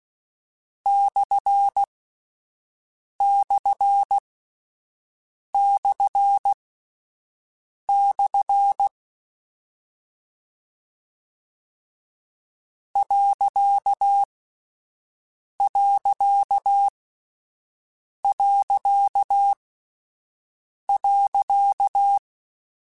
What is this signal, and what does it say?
Morse code, heard as a keyed tone, about 16 wpm, spaced as for 5 wpm, 800 Hz. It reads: //// ....
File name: Misc / Misc-mp3s / Punct @5